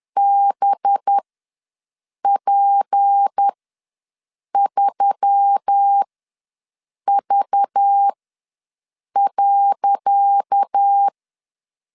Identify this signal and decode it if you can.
Morse code, heard as a keyed tone, about 10 wpm, spaced as for 6 wpm, 800 Hz. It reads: BP3V.
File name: LEZIONI / 10 SIGNS / 10C DOT ws 10wpm 4213